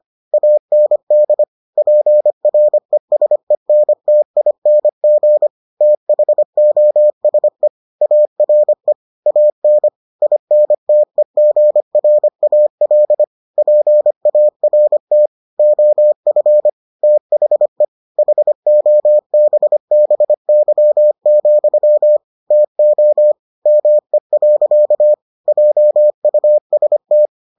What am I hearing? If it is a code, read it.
Morse code, heard as a keyed tone, about 25 wpm, 600 Hz. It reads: AND PRESENTING THOSE ARE AN INTEGRAL PART OF THE HOBBY, TO ME. JUST